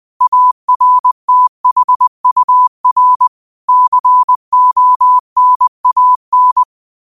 Morse code, heard as a keyed tone, about 20 wpm, 1000 Hz. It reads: ARTHUR CONAN
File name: qotdfiles / 20wpm